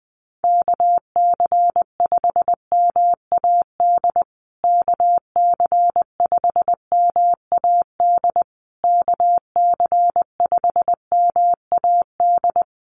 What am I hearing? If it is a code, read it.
Morse code, heard as a keyed tone, about 20 wpm, 700 Hz. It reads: KC5MAD KC5MAD KC5MAD
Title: USA Call Signs Repeated 3x 20wpm